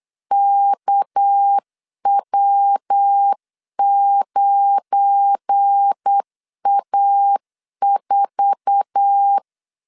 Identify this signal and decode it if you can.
Morse code, heard as a keyed tone, about 8 wpm, 800 Hz. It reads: KW9A4
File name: CW / DET / 1 9 1500C S A 8wpm 5637